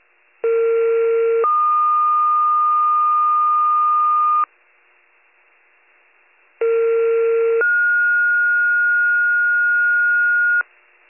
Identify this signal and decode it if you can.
Two-tone page, heard as a one-second tone, then a three-second tone, about 460 then 1220 Hz, about 460 then 1430 Hz. It is heard from 0.4 to 10.6 s.